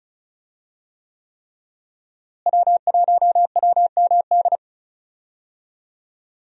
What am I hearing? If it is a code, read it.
Morse code, heard as a keyed tone, about 36 wpm, 700 Hz. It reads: W1WMD
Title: USA Call Signs - 5 Characters - Repeated 3x - 35wpm